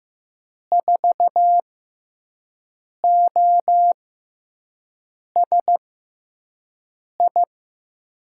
Morse code, heard as a keyed tone, about 15 wpm, spaced as for 5 wpm, 700 Hz. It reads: 4OSI
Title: USA Call Signs - 6x Character Spacing - 15wpm